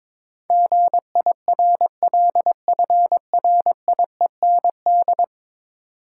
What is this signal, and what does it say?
Morse code, heard as a keyed tone, about 22 wpm, 700 Hz. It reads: GIRLFRIEND
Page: morse-code-ninja-practice-22wpm